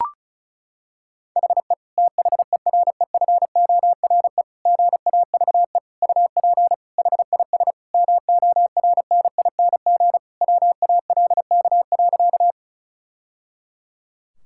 Morse code, heard as a keyed tone, about 34 wpm, 700 Hz. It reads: HE THEREFORE GAVE UP HIS MORNING WALK.
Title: Sentences from Top 1000 Words - Review Part 5 - 35wpm